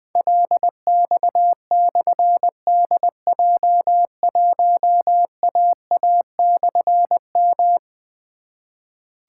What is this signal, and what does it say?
Morse code, heard as a keyed tone, about 20 wpm, 700 Hz. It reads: LX/DJ1AA/M
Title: Difficult Call Signs 20wpm